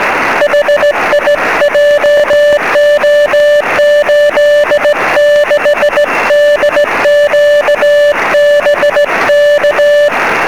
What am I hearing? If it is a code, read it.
Morse code, heard as a keyed tone, about 16 wpm, 590 Hz, with noise. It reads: HI JO86DQ BK
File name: sm7lcb / 06-sk7ca 23cm